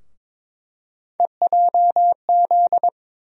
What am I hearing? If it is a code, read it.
Morse code, heard as a keyed tone, about 22 wpm, 700 Hz. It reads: EJZ
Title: USA Call Sign Postfixes - 3 Letters 22wpm